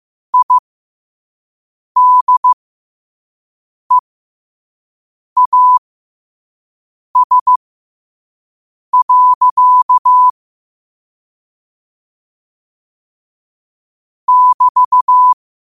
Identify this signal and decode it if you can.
Morse code, heard as a keyed tone, about 15 wpm, spaced as for 5 wpm, 1000 Hz. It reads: IDEAS. <BT>